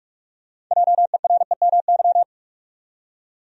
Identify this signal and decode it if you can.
Morse code, heard as a keyed tone, about 47 wpm, 700 Hz. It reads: JEREMY
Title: USA Common Names - 45wpm